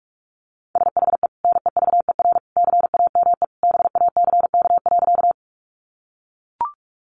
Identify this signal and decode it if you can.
Morse code, heard as a keyed tone, about 44 wpm, 700 Hz. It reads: SHE NEVER CAME BACK.
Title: Sentences from Top 300 Words 45wpm